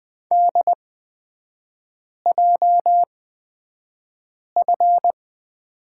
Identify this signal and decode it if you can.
Morse code, heard as a keyed tone, about 20 wpm, spaced as for 5 wpm, 690 Hz. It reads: DJF
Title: 229 Lesson 37e - Letter x - Three Characters 20wpm